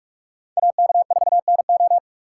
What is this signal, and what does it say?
Morse code, heard as a keyed tone, about 43 wpm, 700 Hz. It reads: AK4NO